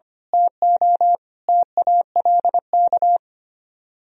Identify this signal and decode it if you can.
Morse code, heard as a keyed tone, about 26 wpm, 700 Hz. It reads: TO TALK
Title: Speed-Racing - Sets of 5 Words 25wpm